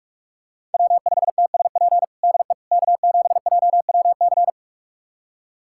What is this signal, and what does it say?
Morse code, heard as a keyed tone, about 46 wpm, 700 Hz. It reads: W5TSP DE K7JWC
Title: USA Callsign DE Callsign - 5 Characters - 45wpm